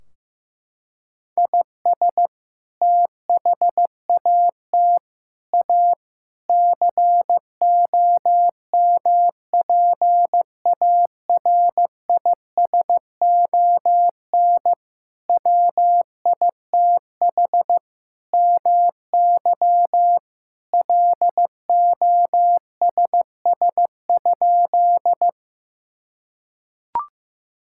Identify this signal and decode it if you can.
Morse code, heard as a keyed tone, about 15 wpm, 700 Hz. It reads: IS THAT A COMPARISON WITH MY LOSS?
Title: Sentences from Top 1600 Words 15wpm